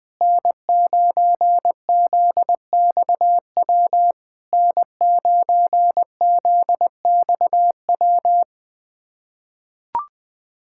Morse code, heard as a keyed tone, about 20 wpm, 690 Hz. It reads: N9ZXW N9ZXW
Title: SOTA QSO Practice 20wpm